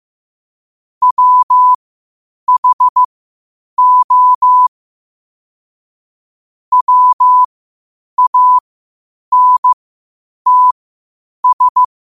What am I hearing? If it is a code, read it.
Morse code, heard as a keyed tone, about 15 wpm, spaced as for 8 wpm, 1000 Hz. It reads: WHO WANTS
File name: qotdfiles / 8wpm